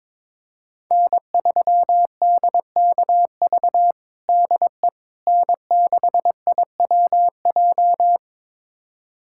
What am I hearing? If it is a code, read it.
Morse code, heard as a keyed tone, about 21 wpm, 700 Hz. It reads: N3DKV DE N6IWJ
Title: USA Callsign DE Callsign - 5 Characters - 22wpm